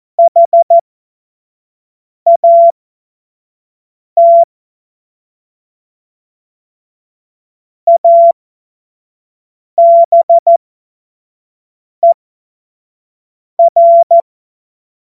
Morse code, heard as a keyed tone, about 14 wpm, spaced as for 5 wpm, 680 Hz. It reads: HAT ABER
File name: Morsekurs / Witze / Witz 09. November 2021, 25 F70